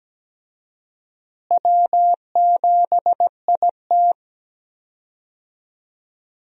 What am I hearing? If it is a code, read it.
Morse code, heard as a keyed tone, about 17 wpm, 700 Hz. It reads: W7IT
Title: USA Call Signs - 4 Characters - Repeated 3x - 17wpm